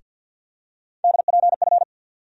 Morse code, heard as a keyed tone, about 52 wpm, 700 Hz. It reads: DPF
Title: USA Call Sign Postfixes - 3 Letters 50wpm